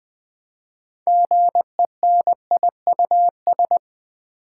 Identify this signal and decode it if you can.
Morse code, heard as a keyed tone, about 20 wpm, 700 Hz. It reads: GENIUS
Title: 6 Letter Words - First 3 Letters Given - 20wpm